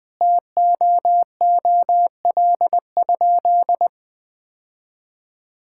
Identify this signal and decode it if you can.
Morse code, heard as a keyed tone, about 20 wpm, 700 Hz. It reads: TOOL?
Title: Sentences from Top 1000 Words - Review Part 6 - 20wpm